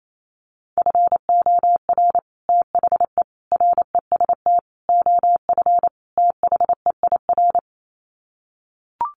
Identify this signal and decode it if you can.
Morse code, heard as a keyed tone, about 28 wpm, 700 Hz. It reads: FOR THE REST OF THEIR
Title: Sets of 5 Words Encore2 28wpm